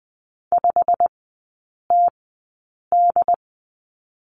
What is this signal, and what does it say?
Morse code, heard as a keyed tone, about 20 wpm, spaced as for 8 wpm, 690 Hz. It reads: 5TD
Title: 087 Lesson 15e - Number 5 - Three Characters 20wpm